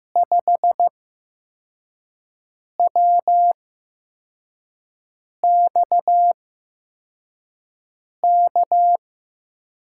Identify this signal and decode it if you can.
Morse code, heard as a keyed tone, about 15 wpm, spaced as for 4 wpm, 700 Hz. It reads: 5WXK
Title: USA Call Signs - 6 Characters - 8x Character Spacing - 15wpm